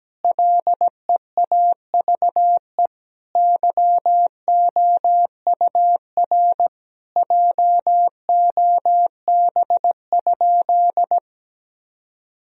Morse code, heard as a keyed tone, about 17 wpm, 700 Hz. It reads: LEAVE YOUR JOB?
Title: Sentences from Top 1000 Words - Review Part 3 - 17wpm